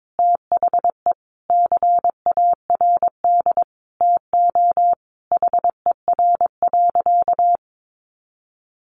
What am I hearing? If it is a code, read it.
Morse code, heard as a keyed tone, about 22 wpm, 700 Hz. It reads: THE CARD TO HER.
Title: Sentences from Top 600 Words 22wpm